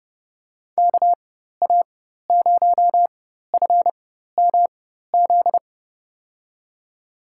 Morse code, heard as a keyed tone, about 30 wpm, spaced as for 14 wpm, 700 Hz. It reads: KA0FMZ
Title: USA Call Signs - 6 Characters - 4x Character Spacing - 30wpm